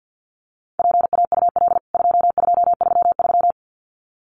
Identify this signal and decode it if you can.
Morse code, heard as a keyed tone, about 49 wpm, 700 Hz. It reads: PAUL 2233
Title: CWT Exchange 50wpm